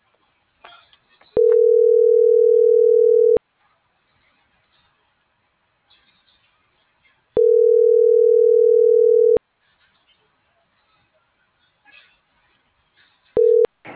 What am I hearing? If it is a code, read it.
Ringback tone (North America), 440 + 480 Hz.